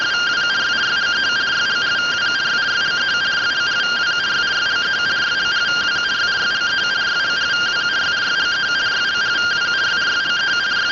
Two-tone teleprinter signal, about 170 Hz shift.
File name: findings / CT 205 ax mips, iDose 3 0004.gen-15 rtty rx